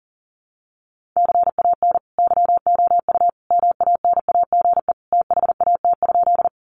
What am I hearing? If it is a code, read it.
Morse code, heard as a keyed tone, about 40 wpm, 700 Hz. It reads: CAN YOU MANAGE THAT?
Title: Sentences from Top 1400 Words 40wpm